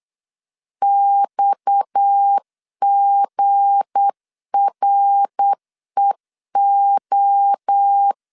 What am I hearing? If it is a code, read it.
Morse code, heard as a keyed tone, about 8 wpm, 800 Hz. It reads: XGREO
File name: LEZIONI / 7 GF9Z / 7 GF9Z 8wpm 3213